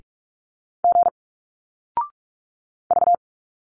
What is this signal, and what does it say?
Morse code, heard as a keyed tone, about 46 wpm, 700 Hz. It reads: G V
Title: Single Letters 45wpm